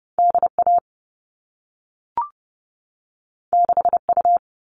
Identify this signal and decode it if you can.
Morse code, heard as a keyed tone, about 30 wpm, 700 Hz. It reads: D A 6 U
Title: All Cut Numbers - 2-Digit 30wpm